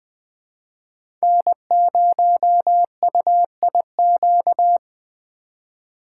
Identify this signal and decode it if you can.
Morse code, heard as a keyed tone, about 20 wpm, 700 Hz. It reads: N0UIQ
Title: USA Call Signs - 5 Characters - 3x Character Spacing - 20wpm